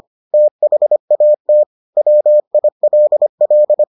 Morse code, heard as a keyed tone, about 25 wpm, 600 Hz. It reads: THAT WILL